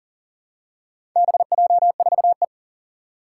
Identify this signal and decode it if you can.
Morse code, heard as a keyed tone, about 40 wpm, 700 Hz. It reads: BJ4E
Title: Four Challenging Letter-Number Combinations 40wpm